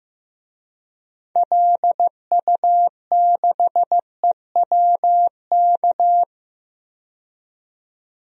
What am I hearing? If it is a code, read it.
Morse code, heard as a keyed tone, about 15 wpm, 700 Hz. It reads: LU6EWK
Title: DXCC-100 Call Signs - 6 Characters - 15wpm